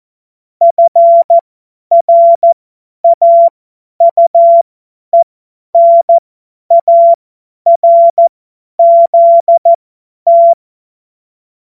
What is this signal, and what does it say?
Morse code, heard as a keyed tone, about 14 wpm, spaced as for 10 wpm, 680 Hz. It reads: FRAUENARZT
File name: Morsekurs / Witze / Witz 05 März 2024 50 F70